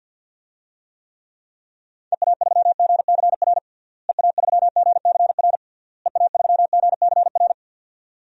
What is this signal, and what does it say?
Morse code, heard as a keyed tone, about 51 wpm, 700 Hz. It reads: EA3GCR EA3GCR EA3GCR
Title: DXCC-100 Call Signs - 6 Characters - Repeated 3x - 50wpm